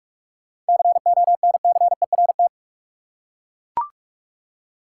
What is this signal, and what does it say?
Morse code, heard as a keyed tone, about 42 wpm, 700 Hz. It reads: CONCERT
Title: Seven Letter Words - 45wpm